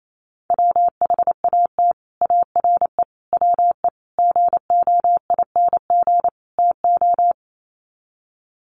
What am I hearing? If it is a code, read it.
Morse code, heard as a keyed tone, about 28 wpm, 700 Hz. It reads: WHAT ARE WE GOING TO
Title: Sets of 5 Words 28wpm